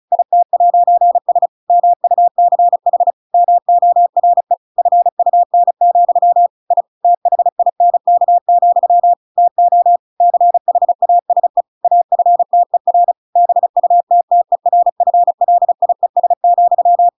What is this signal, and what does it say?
Morse code, heard as a keyed tone, about 34 wpm, 700 Hz. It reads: IT'S MUCH MORE FUN, I THINK, TO CHASE AFTER BUTTERFLIES,